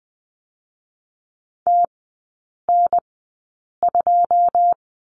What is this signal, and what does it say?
Morse code, heard as a keyed tone, about 20 wpm, spaced as for 8 wpm, 690 Hz. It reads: TN2
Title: 081 Lesson 14e - Number 2 - Three Characters 20wpm